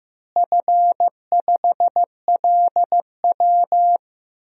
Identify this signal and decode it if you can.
Morse code, heard as a keyed tone, about 15 wpm, 700 Hz. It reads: F5LW
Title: DXCC-100 Call Signs - 4 Characters - Repeated 3x - 15wpm